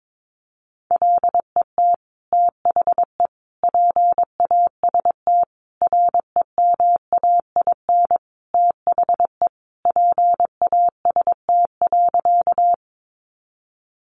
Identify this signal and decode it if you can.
Morse code, heard as a keyed tone, about 22 wpm, 700 Hz. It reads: LET THE PAST REMAIN THE PAST.
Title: Sentences from Top 1200 Words 22wpm